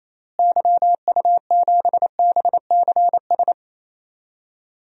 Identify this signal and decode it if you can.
Morse code, heard as a keyed tone, about 28 wpm, 700 Hz. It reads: YU7BCS
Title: DXCC-100 Call Signs - 6 Characters - Repeated 2x - 28wpm